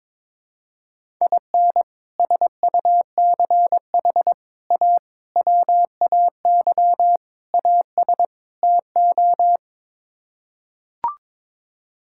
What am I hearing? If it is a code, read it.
Morse code, heard as a keyed tone, about 22 wpm, 700 Hz. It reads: IN SUCH A WAY AS TO
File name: practice / 22wpm / Sets of 6 Words - 22wpm